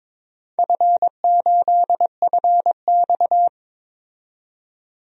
Morse code, heard as a keyed tone, about 22 wpm, 700 Hz. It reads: F8FX
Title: DXCC-100 Call Signs - 4 Characters - 22wpm